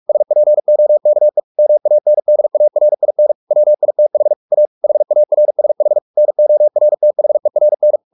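Morse code, heard as a keyed tone, about 44 wpm, 600 Hz. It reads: SPOKE MANDARIN WITH A HARSH NORTHERN
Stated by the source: Morse practice at 45 wpm, the novel Never by Ken Follett in English, from CW med Gnister.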